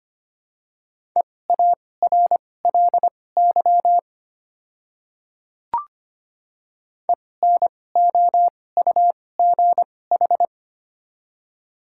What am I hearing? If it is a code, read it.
Morse code, heard as a keyed tone, about 25 wpm, spaced as for 18 wpm, 700 Hz. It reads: EARLY ENOUGH
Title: 2x Character Spacing - Top 500 Words - 25wpm